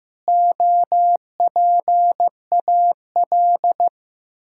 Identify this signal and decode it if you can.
Morse code, heard as a keyed tone, about 14 wpm, 700 Hz. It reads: OPAL
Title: Pangram Challenge v2 - 15wpm